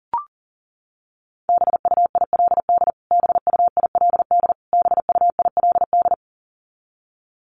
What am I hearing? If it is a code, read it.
Morse code, heard as a keyed tone, about 40 wpm, 700 Hz. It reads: BUILD BUILD BUILD
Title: Top 300 Words Repeated 40wpm